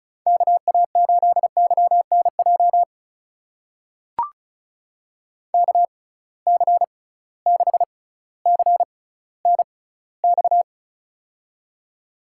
Morse code, heard as a keyed tone, about 36 wpm, spaced as for 18 wpm, 700 Hz. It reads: KA8YNJ KC6CNX
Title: USA Call Signs - 6 Characters - 6x Character Spacing - 35wpm